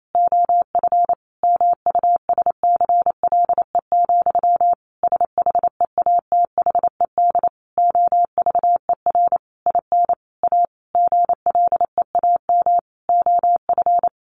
Morse code, heard as a keyed tone, about 28 wpm, 700 Hz. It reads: OF MUSCLE, SHEATHED OVER IN A GLEAM OF
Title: A Sound Of Thunder 28wpm